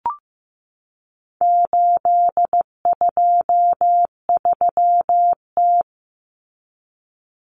Morse code, heard as a keyed tone, about 15 wpm, 700 Hz. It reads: 823T
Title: Common Cut Numbers - 4-Digit 15wpm